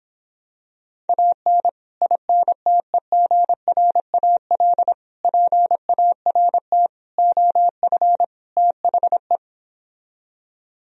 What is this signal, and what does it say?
Morse code, heard as a keyed tone, about 26 wpm, 700 Hz. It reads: AN INTEGRAL PART OF THE